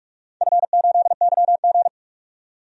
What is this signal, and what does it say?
Morse code, heard as a keyed tone, about 46 wpm, 700 Hz. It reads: F8YG
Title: DXCC-100 Call Signs - 4 Characters - 45wpm